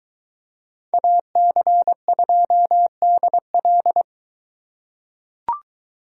Morse code, heard as a keyed tone, about 21 wpm, 700 Hz. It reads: AC2DL